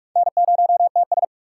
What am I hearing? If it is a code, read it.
Morse code, heard as a keyed tone, about 59 wpm, spaced as for 51 wpm, 700 Hz. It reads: N0TS
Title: Speed Racing - USA 1x2 Call Signs 45wpm